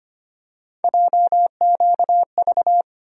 Morse code, heard as a keyed tone, about 25 wpm, 700 Hz. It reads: JQV
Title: USA Call Sign Postfixes - 3 Letters 25wpm